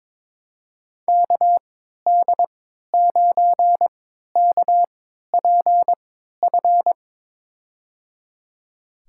Morse code, heard as a keyed tone, about 22 wpm, spaced as for 13 wpm, 700 Hz. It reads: KD9KPF
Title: USA Call Signs - 6 Characters - 3x Character Spacing - 22wpm